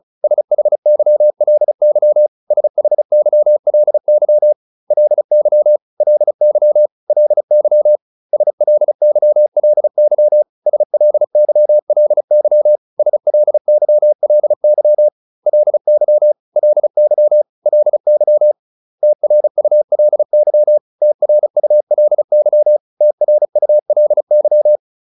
Morse code, heard as a keyed tone, about 34 wpm, 600 Hz. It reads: SHYLY SHYLY LY LY LY SLYLY SLYLY SLYLY LY LY LY TRULY TRULY TRULY
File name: LY5 35